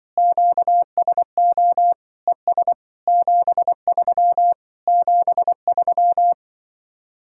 Morse code, heard as a keyed tone, about 24 wpm, 700 Hz. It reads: QSO ES 73 73
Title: Speed-Racing Common QSO Phrases - 20wpm